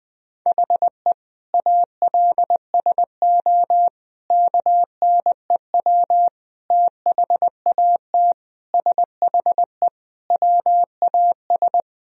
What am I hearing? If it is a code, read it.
Morse code, heard as a keyed tone, about 20 wpm, 700 Hz. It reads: HE ALSO KNEW THAT SHE WAS
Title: Sentences from Top 1000 Words - Review Part 1 - 20wpm